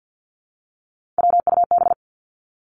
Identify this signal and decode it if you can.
Morse code, heard as a keyed tone, about 50 wpm, 700 Hz. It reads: WUB
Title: Three Challenging Letter-Number Combinations 50wpm